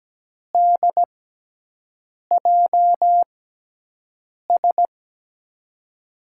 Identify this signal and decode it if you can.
Morse code, heard as a keyed tone, about 17 wpm, spaced as for 6 wpm, 700 Hz. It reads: DJS
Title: USA Call Signs - 6 Characters - 6x Character Spacing - 17wpm